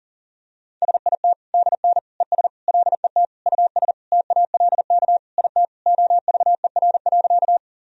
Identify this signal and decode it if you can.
Morse code, heard as a keyed tone, about 40 wpm, 700 Hz. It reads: SIT DN ES LET US TALK IT OVER.
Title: Sentences from Top 500 Words and Abbreviations 40wpm